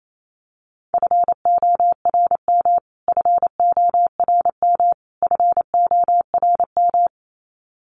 Morse code, heard as a keyed tone, about 28 wpm, 700 Hz. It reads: FORM FORM FORM
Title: Top 300 Words Repeated 28wpm